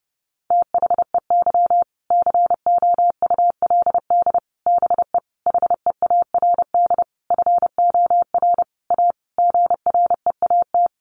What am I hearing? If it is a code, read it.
Morse code, heard as a keyed tone, about 30 wpm, 700 Hz. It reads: THEY COULD BE HEARD FOR A GREAT
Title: Sentences from Top 1200 Words 30wpm